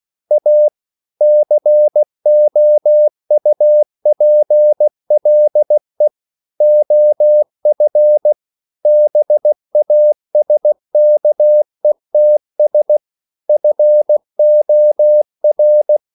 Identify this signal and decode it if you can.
Morse code, heard as a keyed tone, about 16 wpm, 600 Hz. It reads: A COUPLE OF BASKETS FOR